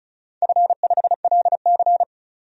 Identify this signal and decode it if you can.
Morse code, heard as a keyed tone, about 36 wpm, 700 Hz. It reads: F5LC
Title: DXCC-100 Call Signs - 4 Characters - 35wpm